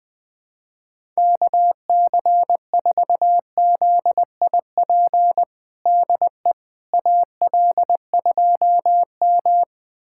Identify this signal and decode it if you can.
Morse code, heard as a keyed tone, about 21 wpm, 700 Hz. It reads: KC4ZIP DE AL2M
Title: USA Callsign DE Callsign 20wpm